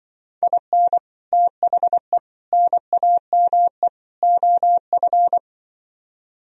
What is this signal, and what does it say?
Morse code, heard as a keyed tone, about 23 wpm, 700 Hz. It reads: IN THE NAME OF
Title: Speed-Racing - Sets of 4 Words 20wpm